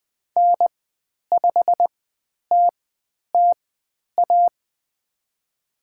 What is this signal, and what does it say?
Morse code, heard as a keyed tone, about 20 wpm, spaced as for 10 wpm, 700 Hz. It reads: N5TTA
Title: 109 Lesson 18g - Letter m - Call Signs 20wpm